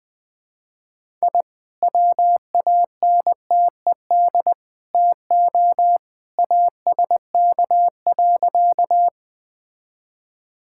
Morse code, heard as a keyed tone, about 20 wpm, 700 Hz. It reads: I WANTED TO ASK.
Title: Sentences with 3-Word Phrases - Spoken Beginning - 20wpm